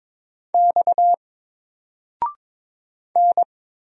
Morse code, heard as a keyed tone, about 22 wpm, 700 Hz. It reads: X N
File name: Single-Letter-Number-22wpm